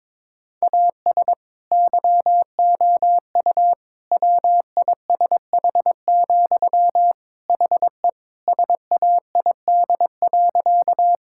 Morse code, heard as a keyed tone, about 21 wpm, 700 Hz. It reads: AS YOU WISH, HE SAID.
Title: Sentences from Top 1600 Words 22wpm